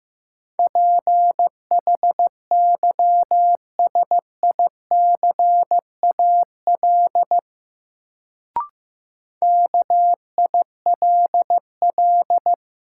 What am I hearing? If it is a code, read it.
Morse code, heard as a keyed tone, about 15 wpm, 700 Hz. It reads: PHYSICAL KILL
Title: Sentences from Top 1400 Words - New Word Drill Alt - 15wpm